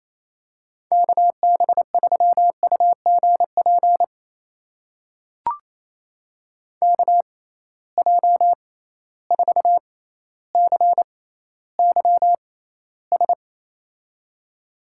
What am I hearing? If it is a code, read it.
Morse code, heard as a keyed tone, about 28 wpm, spaced as for 14 wpm, 700 Hz. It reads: KB3UGP KJ4CYS
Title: USA Call Signs - 6 Characters - 6x Character Spacing - 28wpm